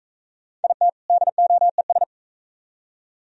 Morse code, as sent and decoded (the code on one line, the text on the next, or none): .. - / -.. --- . ...
IT DOES